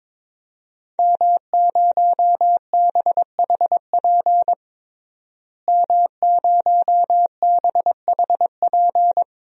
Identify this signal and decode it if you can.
Morse code, heard as a keyed tone, about 22 wpm, 700 Hz. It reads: M0BHP M0BHP
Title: DXCC-100 Call Signs - 5 Characters - Repeated 3x - 22wpm